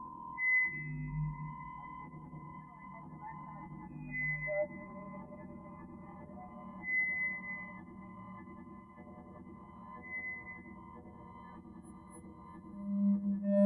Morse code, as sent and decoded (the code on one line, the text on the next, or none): none